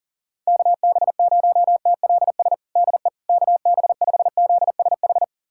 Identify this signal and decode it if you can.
Morse code, heard as a keyed tone, about 37 wpm, 700 Hz. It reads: KB0TLS DE KB5ZSH
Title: USA Callsign DE Callsign - 6 Characters - 40wpm